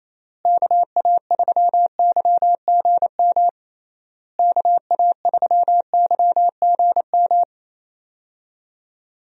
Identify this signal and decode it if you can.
Morse code, heard as a keyed tone, about 27 wpm, 700 Hz. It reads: KA3YGM KA3YGM
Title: USA Call Signs - 6 Characters - Repeated 3x - 28wpm